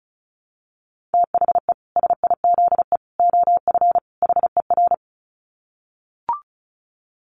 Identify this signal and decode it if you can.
Morse code, heard as a keyed tone, about 34 wpm, 700 Hz. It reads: THE SIZE OF HER
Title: Dit Challenge 35wpm